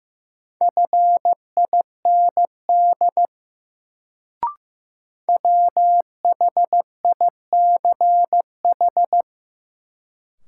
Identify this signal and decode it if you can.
Morse code, heard as a keyed tone, about 15 wpm, 700 Hz. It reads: FIND WHICH
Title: Sentences from Top 100 Words - New Word Drill - 15wpm